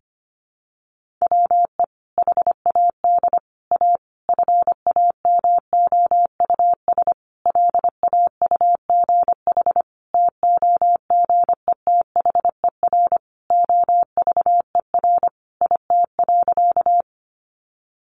Morse code, heard as a keyed tone, about 25 wpm, 700 Hz. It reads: WE HAD A FAMOUS LAUGH TOGETHER OVER IT.
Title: Sentences from Top 1500 Words 25wpm